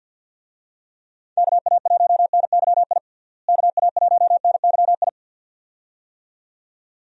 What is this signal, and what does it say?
Morse code, heard as a keyed tone, about 47 wpm, 700 Hz. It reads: KA1NYI KA1NYI
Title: USA Call Signs - 6 Characters - Repeated 2x - 50wpm